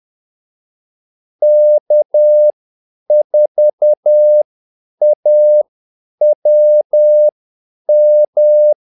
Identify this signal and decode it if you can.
Morse code, heard as a keyed tone, about 10 wpm, spaced as for 8 wpm, 600 Hz. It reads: K4AWM
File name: Callsigns-10-8wpm